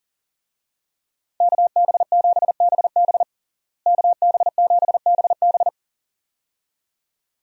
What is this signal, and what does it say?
Morse code, heard as a keyed tone, about 37 wpm, 700 Hz. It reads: KB7BB KB7BB
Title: USA Call Signs - 5 Characters - Repeated 2x - 40wpm